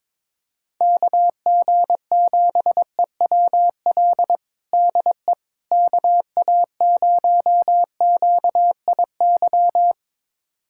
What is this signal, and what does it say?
Morse code, heard as a keyed tone, about 22 wpm, 700 Hz. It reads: KG7EWL DE KA0QIY
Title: USA Callsign DE Callsign - 6 Characters - 22wpm